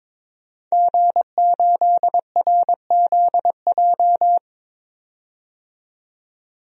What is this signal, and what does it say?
Morse code, heard as a keyed tone, about 22 wpm, 700 Hz. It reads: G8RZJ
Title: DXCC-100 Call Signs - 5 Characters - 22wpm